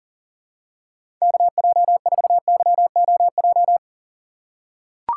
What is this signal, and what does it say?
Morse code, heard as a keyed tone, about 40 wpm, 700 Hz. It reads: KJ4YOJ
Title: USA Call Signs - 6 Characters - 4x Character Spacing - 40wpm